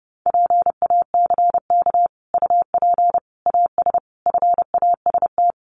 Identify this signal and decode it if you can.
Morse code, heard as a keyed tone, about 30 wpm, 700 Hz. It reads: PACK UP AS FAST
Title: Sentences from Top 1300 Words 30wpm